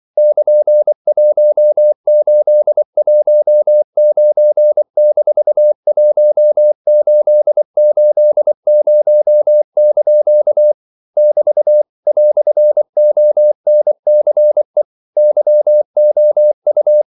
Morse code, heard as a keyed tone, about 24 wpm, 600 Hz. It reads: <KN>1819-1880) <BT> "ONCE YOU
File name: CW / Practice / EnglishQuotations0224